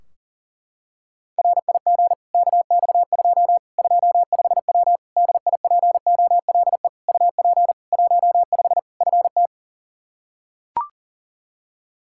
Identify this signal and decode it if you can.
Morse code, heard as a keyed tone, about 40 wpm, 700 Hz. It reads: RIG KX2 25W DIPOLE UP 15 FT
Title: Practice QSO - Exchange 2 - Info - 40wpm